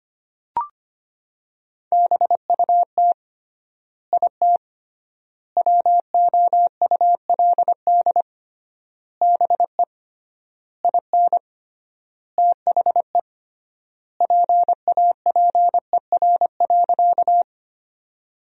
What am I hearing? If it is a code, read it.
Morse code, heard as a keyed tone, about 25 wpm, 700 Hz. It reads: BUT IT WOULD BE IN THE PAPER.
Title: 3x Word-Spacing - Sentences from Top 300 Words - 25wpm